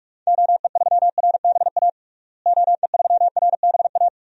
Morse code, heard as a keyed tone, about 46 wpm, 700 Hz. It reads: OE3RBA OE3RBA
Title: DXCC-100 Call Signs - 6 Characters - Repeated 3x - 45wpm